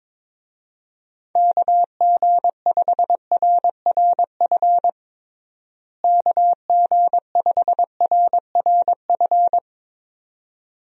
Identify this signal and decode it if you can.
Morse code, heard as a keyed tone, about 21 wpm, 700 Hz. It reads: KG5RRF KG5RRF
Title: USA Call Signs - 6 Characters - Repeated 2x - 22wpm